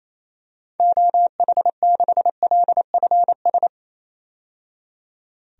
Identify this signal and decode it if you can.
Morse code, heard as a keyed tone, about 28 wpm, 700 Hz. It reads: OH6LFS
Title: DXCC-100 Call Signs - 6 Characters - 28wpm